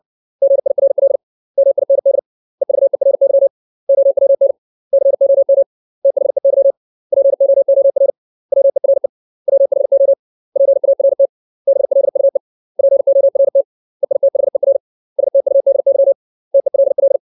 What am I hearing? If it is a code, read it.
Morse code, heard as a keyed tone, about 60 wpm, 550 Hz. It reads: GIRL MIND EVERY OWN COM THEY POOR WERE WHO PART BLUE PORT EITHER STUDY TELL
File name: 632 words / 632 Words 60WPM